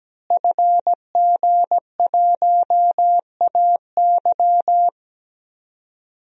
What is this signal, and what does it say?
Morse code, heard as a keyed tone, about 17 wpm, 700 Hz. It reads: FG1AY
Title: DXCC-100 Call Signs - 5 Characters - Repeated 3x - 17wpm